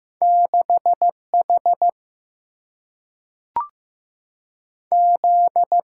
Morse code, heard as a keyed tone, about 15 wpm, 700 Hz. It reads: 6H Z
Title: Single-Letter Single-Number 15wpm